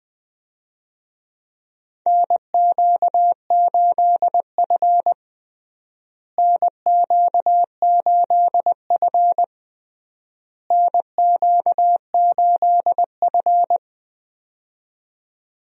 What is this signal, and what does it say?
Morse code, heard as a keyed tone, about 21 wpm, 700 Hz. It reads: NQ8F NQ8F NQ8F
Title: USA Call Signs - 4 Characters - Repeated 3x - 20wpm